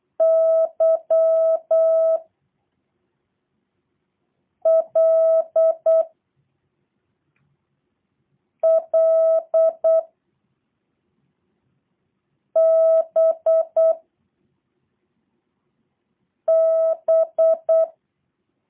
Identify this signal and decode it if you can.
Morse code, heard as a keyed tone, about 8 wpm, 650 Hz. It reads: Y L L B B
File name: jh8cbh / 2023cw / review1-6